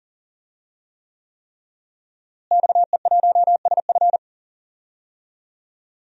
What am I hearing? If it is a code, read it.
Morse code, heard as a keyed tone, about 40 wpm, 700 Hz. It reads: XE1SF
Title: DXCC-100 Call Signs - 5 Characters - Repeated 3x - 40wpm